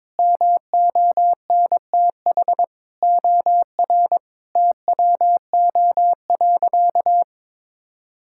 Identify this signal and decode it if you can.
Morse code, heard as a keyed tone, about 21 wpm, 700 Hz. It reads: MONTH OR TWO.
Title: Sentences from Top 1000 Words - Review Part 5 - 22wpm